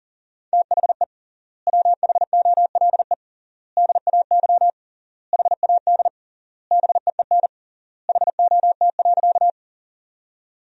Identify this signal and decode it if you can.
Morse code, heard as a keyed tone, about 40 wpm, 700 Hz. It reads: THE WHOLE DAY HAD BEEN HOT.
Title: 3x Word-Spacing - Sentences from Top 500 Words - 40wpm